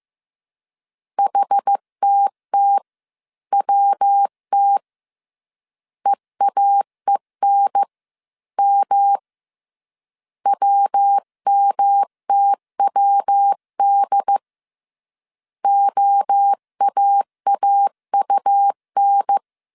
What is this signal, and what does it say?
Morse code, heard as a keyed tone, about 15 wpm, 800 Hz. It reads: HTTWT EAENM WMTWD OAAUN
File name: CW / DET / 1 3 1500C S A 14wpm 2447